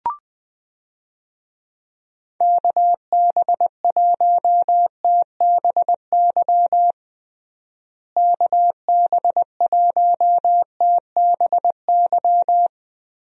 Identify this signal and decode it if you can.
Morse code, heard as a keyed tone, about 20 wpm, 700 Hz. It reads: KB1TBY KB1TBY
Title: USA Call Signs - 6 Characters - Repeated 3x - 20wpm